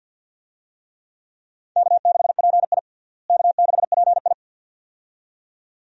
Morse code, heard as a keyed tone, about 47 wpm, 690 Hz. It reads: K6PI K6PI
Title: USA Call Signs - 4 Characters - Repeated 2x - 50wpm